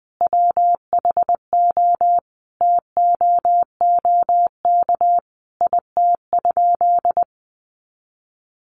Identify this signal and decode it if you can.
Morse code, heard as a keyed tone, about 20 wpm, 700 Hz. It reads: WHO TOOK IT?
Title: Sentences from Top 300 Words 20wpm